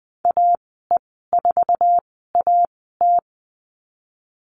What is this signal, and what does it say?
Morse code, heard as a keyed tone, about 20 wpm, spaced as for 14 wpm, 700 Hz. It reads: AE4AT
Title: 055 Lesson 10g - Letter r - Call Signs 20wpm